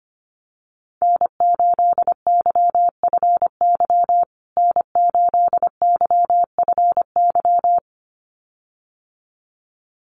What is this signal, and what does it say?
Morse code, heard as a keyed tone, about 25 wpm, 700 Hz. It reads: N8YFY N8YFY
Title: USA Call Signs Repeated 2x 25wpm